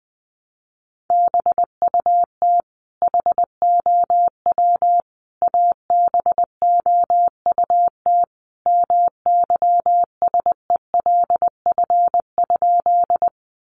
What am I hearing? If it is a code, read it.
Morse code, heard as a keyed tone, about 20 wpm, 700 Hz. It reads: BUT HOW ABOUT MYSELF?
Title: Sentences from Top 1300 Words 20wpm